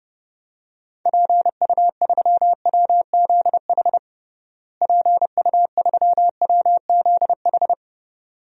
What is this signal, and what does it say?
Morse code, heard as a keyed tone, about 29 wpm, 700 Hz. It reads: PU3WZH PU3WZH
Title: DXCC-100 Call Signs - 6 Characters - Repeated 2x - 30wpm